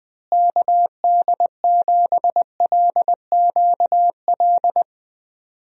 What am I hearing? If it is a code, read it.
Morse code, heard as a keyed tone, about 20 wpm, 700 Hz. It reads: KD7LQL
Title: USA Call Signs - 6 Characters - Repeated 2x - 20wpm